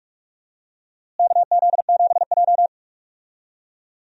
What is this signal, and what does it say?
Morse code, heard as a keyed tone, about 46 wpm, 700 Hz. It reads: KZ7J
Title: USA Call Signs - 4 Characters - Repeated 3x - 45wpm